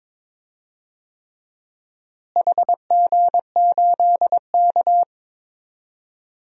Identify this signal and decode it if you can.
Morse code, heard as a keyed tone, about 22 wpm, 700 Hz. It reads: HG8K
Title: DXCC-100 Call Signs - 4 Characters - Repeated 3x - 22wpm